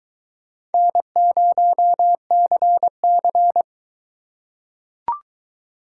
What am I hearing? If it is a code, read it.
Morse code, heard as a keyed tone, about 20 wpm, 700 Hz. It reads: N0CC